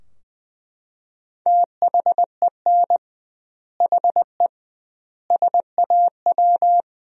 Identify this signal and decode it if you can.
Morse code, heard as a keyed tone, about 20 wpm, 700 Hz. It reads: THEN HE SAW
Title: 2x Word-Spacing - Sentences from Top 400 Words - 20wpm